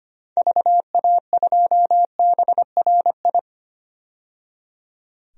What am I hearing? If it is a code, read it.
Morse code, heard as a keyed tone, about 25 wpm, 700 Hz. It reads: VA2BRI
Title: DXCC-100 Call Signs - 6 Characters - 25wpm